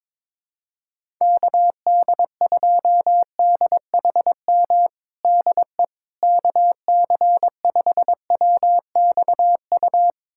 Morse code, heard as a keyed tone, about 22 wpm, 700 Hz. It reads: KD2DHM DE KC5WXU
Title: USA Callsign DE Callsign - 6 Characters - 22wpm